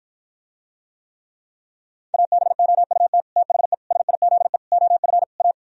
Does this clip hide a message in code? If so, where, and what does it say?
Yes: Morse code from 2.1 to 5.5 s, heard as a keyed tone, about 54 wpm, 700 Hz. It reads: ABOUT THE SIZE OF A